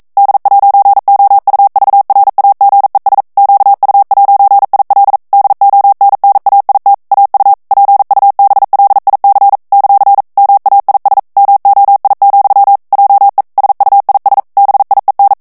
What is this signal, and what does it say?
Morse code, heard as a keyed tone, about 42 wpm, 800 Hz. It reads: D'OUVRAGES QU'IL DONNAIT AU PUBLIC ; MAIS MOI, JE SUIS BIEN
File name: CW / FICHIERS-MP3 / FRANCAIS TEXTE CLAIR 2 VITESSE 40 MOTS MINUTES